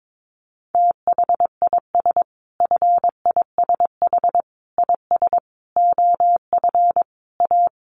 Morse code, heard as a keyed tone, about 22 wpm, 700 Hz. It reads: THIS FISH IS OF A
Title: Sentences from Top 1500 Words 22wpm